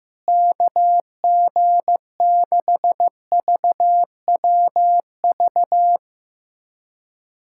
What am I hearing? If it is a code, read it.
Morse code, heard as a keyed tone, about 15 wpm, 700 Hz. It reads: KG6VWV
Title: USA Callsign DE Callsign - 6 Characters - 15wpm